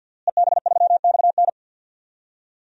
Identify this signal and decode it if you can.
Morse code, heard as a keyed tone, about 46 wpm, 700 Hz. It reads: EB3XN
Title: DXCC-100 Call Signs - 5 Characters - Repeated 3x - 50wpm